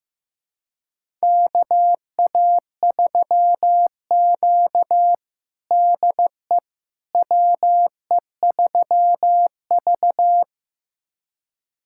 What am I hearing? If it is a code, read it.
Morse code, heard as a keyed tone, about 15 wpm, 700 Hz. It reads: KA3Q DE WE3V
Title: USA Callsign DE Callsign - 4 Characters - 15wpm